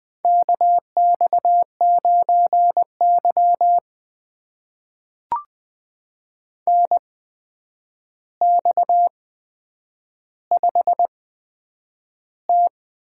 Morse code, heard as a keyed tone, about 20 wpm, spaced as for 9 wpm, 700 Hz. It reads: KX9Y NX5T
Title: USA Call Signs - 4 Characters - 8x Character Spacing - 20wpm